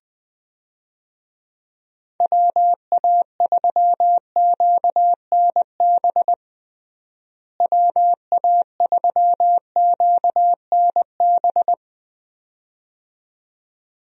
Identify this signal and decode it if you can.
Morse code, heard as a keyed tone, about 21 wpm, 700 Hz. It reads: WA3QNB WA3QNB
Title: USA Call Signs - 6 Characters - Repeated 2x - 20wpm